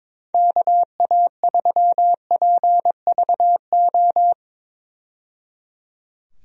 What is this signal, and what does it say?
Morse code, heard as a keyed tone, about 22 wpm, 700 Hz. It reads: KA3PVO
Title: USA Call Signs - 6 Characters - 22wpm